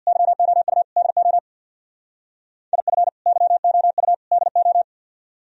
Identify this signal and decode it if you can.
Morse code, heard as a keyed tone, about 59 wpm, 700 Hz. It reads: YOU DO IF YOU DO
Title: Speed-Racing - Sets of 3 Words 50wpm